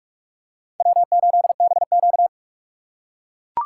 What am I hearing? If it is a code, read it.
Morse code, heard as a keyed tone, about 44 wpm, 700 Hz. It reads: W8BQ
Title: USA Call Signs - 4 Characters - 3x Character Spacing - 45wpm